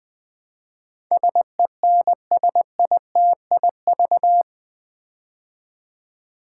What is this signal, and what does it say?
Morse code, heard as a keyed tone, about 20 wpm, 700 Hz. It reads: SENSITIV
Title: Build A Word - 4 Letters At A Time - 11-Letter Words - 20wpm